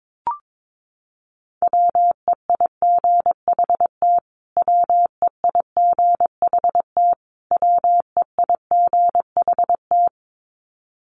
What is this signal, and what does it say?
Morse code, heard as a keyed tone, about 22 wpm, 700 Hz. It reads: WEIGHT WEIGHT WEIGHT
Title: Top 500 Words Repeated 22wpm